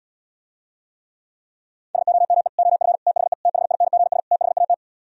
Morse code, heard as a keyed tone, about 77 wpm, 700 Hz. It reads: SOME OF THE THINGS THAT